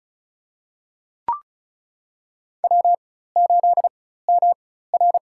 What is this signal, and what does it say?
Morse code, heard as a keyed tone, about 34 wpm, spaced as for 16 wpm, 700 Hz. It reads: W8MR
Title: USA Call Signs - 4 Characters - 4x Character Spacing - 35wpm